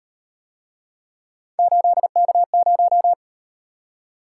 Morse code, heard as a keyed tone, about 37 wpm, 700 Hz. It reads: 8K0